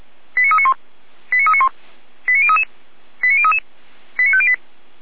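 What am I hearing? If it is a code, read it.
Five-tone selective call (ZVEI), heard as quick runs of five tones, tones of about 70 ms. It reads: 89391 89391 89033 89033 89508
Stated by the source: ZVEI five-tone selective call received by SDR.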